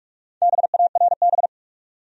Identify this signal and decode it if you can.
Morse code, heard as a keyed tone, about 47 wpm, 700 Hz. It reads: BARB
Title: USA Common Names - Max 4 Letters - 45wpm